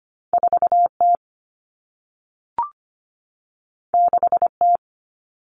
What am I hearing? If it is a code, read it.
Morse code, heard as a keyed tone, about 25 wpm, 700 Hz. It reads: 4T 6T
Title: All Cut Numbers - 2-Digit 25wpm